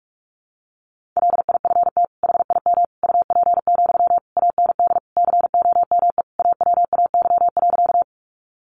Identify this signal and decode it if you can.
Morse code, heard as a keyed tone, about 44 wpm, 700 Hz. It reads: LIFT HIM UP, AND COME AWAY.
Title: Sentences from Top 1000 Words 45wpm